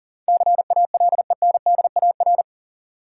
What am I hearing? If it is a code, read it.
Morse code, heard as a keyed tone, about 38 wpm, 700 Hz. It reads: CALENDAR
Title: Sentences from Top 1600 Words - New Word Drill - 40wpm